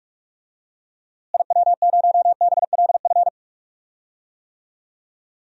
Morse code, heard as a keyed tone, about 46 wpm, 700 Hz. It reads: IW0BLF
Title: DXCC-100 Call Signs - 6 Characters - Repeated 3x - 45wpm